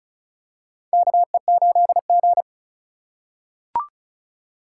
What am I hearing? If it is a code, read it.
Morse code, heard as a keyed tone, about 36 wpm, 700 Hz. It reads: KE8G
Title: USA Call Signs - 4 Characters - 3x Character Spacing - 35wpm